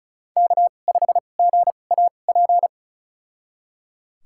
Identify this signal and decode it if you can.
Morse code, heard as a keyed tone, about 36 wpm, spaced as for 25 wpm, 700 Hz. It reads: K5GAP
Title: USA Call Signs - 5 Characters - 2x Character Spacing - 35wpm